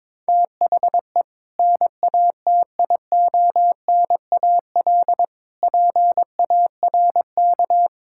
Morse code, heard as a keyed tone, about 21 wpm, 700 Hz. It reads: THE NATIONAL PARK
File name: practice / 22wpm / Sentences with 2-Word Phrases - 22wpm